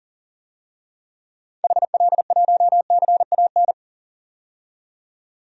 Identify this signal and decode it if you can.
Morse code, heard as a keyed tone, about 40 wpm, 700 Hz. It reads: HL1CAN
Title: DXCC-100 Call Signs - 6 Characters - 40wpm